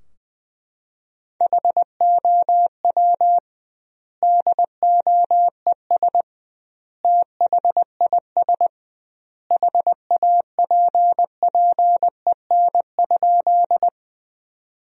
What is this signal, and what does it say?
Morse code, heard as a keyed tone, about 20 wpm, 700 Hz. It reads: HOW DOES THIS HAPPEN?
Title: 2x Word-Spacing - Sentences from Top 400 Words - 20wpm